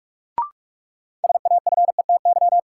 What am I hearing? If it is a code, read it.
Morse code, heard as a keyed tone, about 42 wpm, 700 Hz. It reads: SAFETY
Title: Sentences from Top 700 Words - New Word Drill - 45wpm